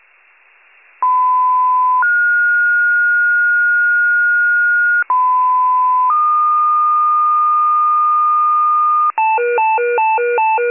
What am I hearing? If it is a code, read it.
Two-tone page, heard as a one-second tone, then a three-second tone, about 1010 then 1430 Hz, about 1010 then 1220 Hz, with noise. It is heard from 1.0 to 9.1 s.
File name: wcalert